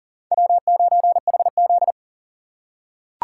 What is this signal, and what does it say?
Morse code, heard as a keyed tone, about 43 wpm, 700 Hz. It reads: W9HZ
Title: USA Call Signs - 4 Characters - 6x Character Spacing - 40wpm